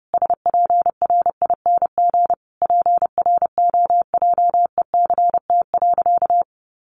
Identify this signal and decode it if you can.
Morse code, heard as a keyed tone, about 30 wpm, 700 Hz. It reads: SPRING PROJECT.
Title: Sentences from Top 1400 Words 30wpm